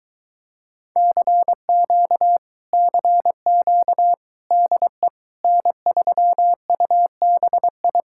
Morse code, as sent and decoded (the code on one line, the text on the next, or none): -.-. --.- / -.-. --.- / -.. / -. ...-- ..- -... ..
CQ CQ DE N3UBI